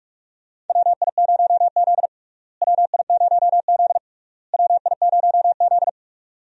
Morse code, heard as a keyed tone, about 46 wpm, 700 Hz. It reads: WI0Z WI0Z WI0Z
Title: USA Call Signs - 4 Characters - Repeated 3x - 45wpm